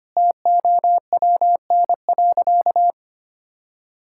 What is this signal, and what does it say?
Morse code, heard as a keyed tone, about 25 wpm, 700 Hz. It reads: TOWN.
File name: practice / 25wpm / Sentences with 3-Word Phrases - 25wpm